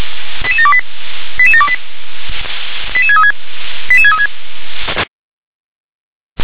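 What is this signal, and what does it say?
Five-tone selective call (ZVEI), heard as quick runs of five tones, tones of about 70 ms. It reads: 88728 88728 88636 88636
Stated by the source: Five-tone ZVEI selective call received on SDR.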